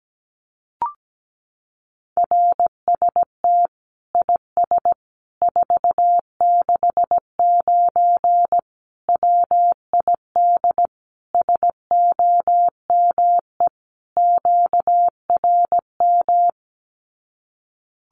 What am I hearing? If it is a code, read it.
Morse code, heard as a keyed tone, about 17 wpm, 700 Hz. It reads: RST IS 469 WID SOME QRM
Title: Common QSO Phrases 17wpm